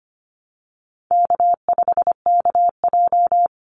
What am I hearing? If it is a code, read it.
Morse code, heard as a keyed tone, about 25 wpm, 700 Hz. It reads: K5KJ
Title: Call Signs - 25wpm